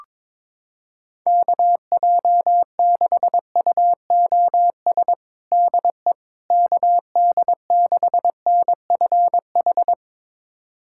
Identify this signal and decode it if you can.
Morse code, heard as a keyed tone, about 22 wpm, 700 Hz. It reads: KJ6UOS DE KD6NFH
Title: USA Callsign DE Callsign - 6 Characters - 22wpm